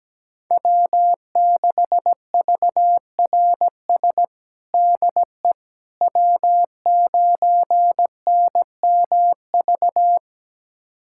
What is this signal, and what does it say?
Morse code, heard as a keyed tone, about 17 wpm, 700 Hz. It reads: W6VRS DE W9NMV
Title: USA Callsign DE Callsign - 5 Characters - 17wpm